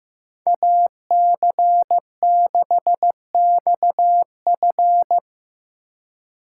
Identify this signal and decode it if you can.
Morse code, heard as a keyed tone, about 15 wpm, 700 Hz. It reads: AC6XF
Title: USA Call Signs - 5 Characters - 2x Character Spacing - 15wpm